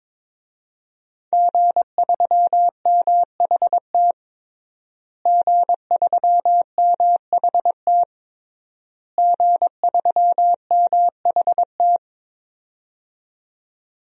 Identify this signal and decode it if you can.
Morse code, heard as a keyed tone, about 22 wpm, 700 Hz. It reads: G3MHT G3MHT G3MHT
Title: DXCC-100 Call Signs - 5 Characters - Repeated 3x - 22wpm